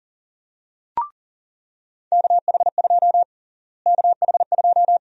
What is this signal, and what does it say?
Morse code, heard as a keyed tone, about 40 wpm, 700 Hz. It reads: KH2 KH2
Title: USA Call Sign Prefixes - Repeated 2x - 40wpm